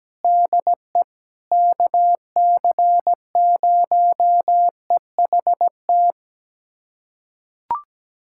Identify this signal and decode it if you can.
Morse code, heard as a keyed tone, about 17 wpm, 700 Hz. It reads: DE KC0EHT
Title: USA Callsign DE Callsign 17wpm